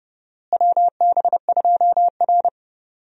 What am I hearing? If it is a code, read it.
Morse code, heard as a keyed tone, about 30 wpm, 700 Hz. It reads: WB2R
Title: USA Call Signs - 4 Characters - Repeated 3x - 30wpm